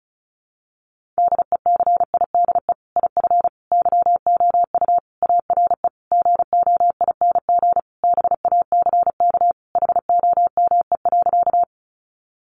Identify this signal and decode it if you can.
Morse code, heard as a keyed tone, about 34 wpm, 700 Hz. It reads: DECIDE IF YOU ARE GOING BACK HOME.
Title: Sentences from Top 500 Words 35wpm